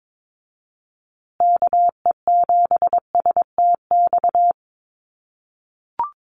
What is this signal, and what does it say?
Morse code, heard as a keyed tone, about 22 wpm, 700 Hz. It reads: KE7STX
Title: USA Call Signs Repeated 2x 22wpm